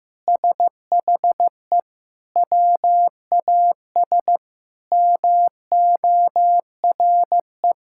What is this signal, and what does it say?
Morse code, heard as a keyed tone, about 16 wpm, 700 Hz. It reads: SHE WAS MORE
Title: Sentences from Top 1000 Words - Review Part 6 - 15wpm